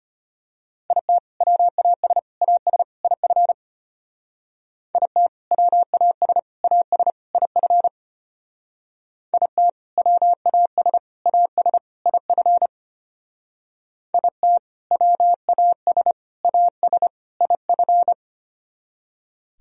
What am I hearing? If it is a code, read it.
Morse code, heard as a keyed tone, about 31 wpm, 700 Hz. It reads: IT WAS AS IF IT WAS AS IF IT WAS AS IF IT WAS AS IF
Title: Speed-Racing - Sets of 4 Words 25wpm